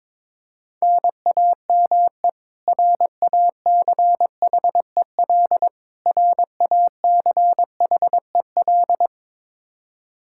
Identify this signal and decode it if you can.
Morse code, heard as a keyed tone, about 22 wpm, 700 Hz. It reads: NAME RACHEL RACHEL
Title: Practice QSO - Exchange 1 - Beginning - 22wpm